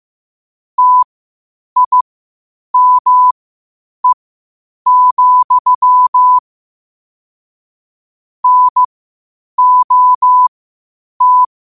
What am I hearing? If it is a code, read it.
Morse code, heard as a keyed tone, about 15 wpm, spaced as for 8 wpm, 1000 Hz. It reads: TIME, NOT
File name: qotdfiles / 8wpm